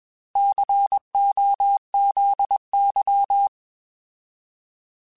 Morse code, heard as a keyed tone, about 21 wpm, 800 Hz. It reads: COZY
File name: Alphabetical / Alphabetical mp3s / Cs @20